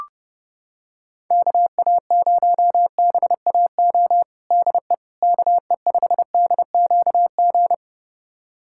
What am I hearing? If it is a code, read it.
Morse code, heard as a keyed tone, about 32 wpm, 700 Hz. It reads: KA0BAO DE KE5DQG
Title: USA Callsign DE Callsign 30wpm